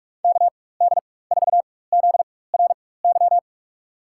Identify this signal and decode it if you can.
Morse code, heard as a keyed tone, about 46 wpm, spaced as for 21 wpm, 700 Hz. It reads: KD4ZRY
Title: USA Call Signs - 6 Characters - 4x Character Spacing - 45wpm